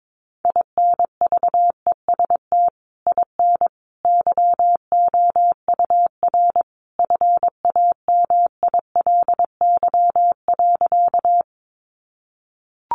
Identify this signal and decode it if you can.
Morse code, heard as a keyed tone, about 22 wpm, 700 Hz. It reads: INVEST IN YOUR FAMILY.
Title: Sentences from Top 1100 Words 22wpm